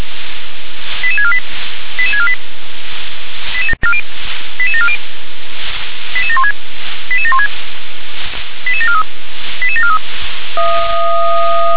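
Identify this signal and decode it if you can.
Five-tone selective call (ZVEI), heard as quick runs of five tones, tones of about 70 ms. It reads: 88648 88648 88740 88716 88716 88743 88743
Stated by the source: ZVEI five-tone call received by SDR.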